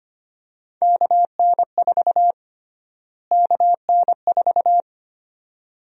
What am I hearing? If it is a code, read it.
Morse code, heard as a keyed tone, about 25 wpm, 700 Hz. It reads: KN4 KN4
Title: USA Call Sign Prefixes - Repeated 2x - 25wpm